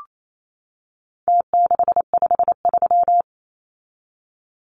Morse code, heard as a keyed tone, about 28 wpm, 700 Hz. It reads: T653
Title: Numbers - Contest Style - 4-Digit 28wpm